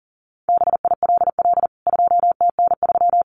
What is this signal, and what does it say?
Morse code, heard as a keyed tone, about 40 wpm, 700 Hz. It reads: BILL 2TN3
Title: CWT Exchange 40wpm